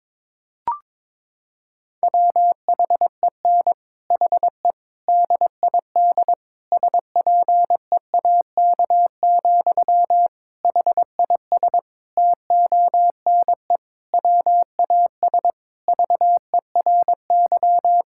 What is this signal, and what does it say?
Morse code, heard as a keyed tone, about 22 wpm, 700 Hz. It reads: WHEN HE DID SPEAK, HIS TONE WAS VERY
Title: Sentences from Top 1600 Words 22wpm